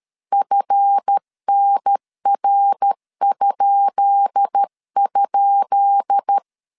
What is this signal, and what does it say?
Morse code, heard as a keyed tone, about 13 wpm, 800 Hz. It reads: FNR??
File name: LEZIONI / 10 SIGNS / 10B INT 12wpm 2414